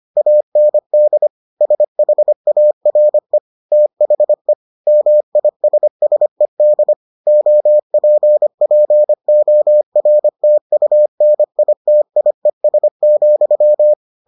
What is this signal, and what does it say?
Morse code, heard as a keyed tone, about 25 wpm, 600 Hz. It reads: AND SHARE THE MISSED OPPORTUNITIES,